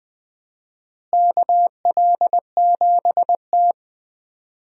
Morse code, heard as a keyed tone, about 21 wpm, 700 Hz. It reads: KL7T
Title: USA Call Signs - 4 Characters - Repeated 2x - 20wpm